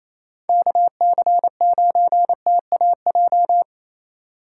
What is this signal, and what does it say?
Morse code, heard as a keyed tone, about 28 wpm, 700 Hz. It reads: KC9TAJ
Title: USA Call Signs - 6 Characters - 4x Character Spacing - 28wpm